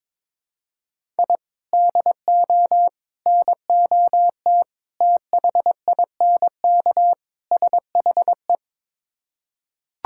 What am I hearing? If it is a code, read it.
Morse code, heard as a keyed tone, about 22 wpm, 700 Hz. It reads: I DO NOT THINK SHE
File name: practice / 22wpm / Build A Phrase - 5-Word Phrases - 22wpm